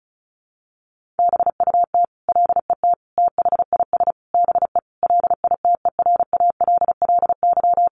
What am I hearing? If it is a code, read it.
Morse code, heard as a keyed tone, about 34 wpm, 700 Hz. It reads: BUT LET THIS BE LITERALLY
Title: Sentences from Top 1400 Words 35wpm